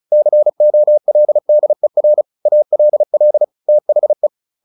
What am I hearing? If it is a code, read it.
Morse code, heard as a keyed tone, about 34 wpm, 600 Hz. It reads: COLDER ALL THE
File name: the old man and the sea 2 35wpm